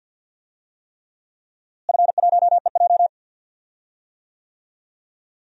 Morse code, heard as a keyed tone, about 52 wpm, 700 Hz. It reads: F1EJ
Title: DXCC-100 Call Signs - 4 Characters - Repeated 3x - 50wpm